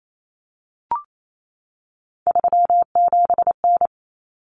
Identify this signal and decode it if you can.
Morse code, heard as a keyed tone, about 28 wpm, 700 Hz. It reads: 37N
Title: Common Cut Numbers - 3-Digit 28wpm